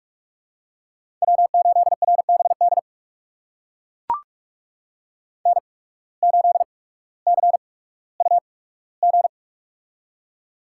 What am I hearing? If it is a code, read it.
Morse code, heard as a keyed tone, about 42 wpm, spaced as for 19 wpm, 700 Hz. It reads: W8RBD N8CUG
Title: USA Call Signs - 5 Characters - 8x Character Spacing - 45wpm